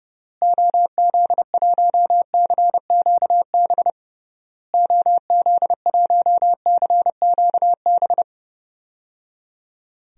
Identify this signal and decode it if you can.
Morse code, heard as a keyed tone, about 30 wpm, 700 Hz. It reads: OZ1CQB OZ1CQB
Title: DXCC-100 Call Signs - 6 Characters - Repeated 3x - 30wpm